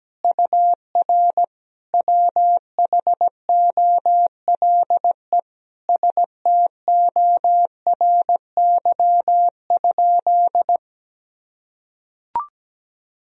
Morse code, heard as a keyed tone, about 17 wpm, 700 Hz. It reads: UR WHOLE STORY?
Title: Sentences from Top 400 Words and Abbreviations 17wpm